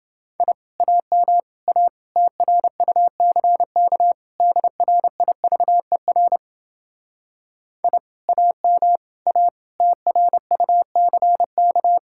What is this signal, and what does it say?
Morse code, heard as a keyed tone, about 30 wpm, 700 Hz. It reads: I AM A TRUCK DRIVER I AM A TRUCK